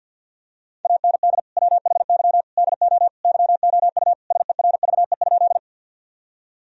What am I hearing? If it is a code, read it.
Morse code, heard as a keyed tone, about 53 wpm, 700 Hz. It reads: AND WHY DO YOU SERVE?
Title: Sentences from Top 1000 Words - Review Part 2 - 50wpm